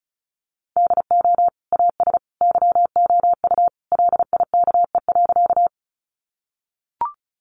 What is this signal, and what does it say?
Morse code, heard as a keyed tone, about 34 wpm, 700 Hz. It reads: DO AS YOU LIKE.
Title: Sentences from Top 100 Words 35wpm